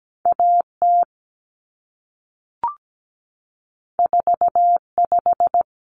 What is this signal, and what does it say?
Morse code, heard as a keyed tone, about 17 wpm, 700 Hz. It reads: AT 45
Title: Common Cut Numbers - 2-Digit 17wpm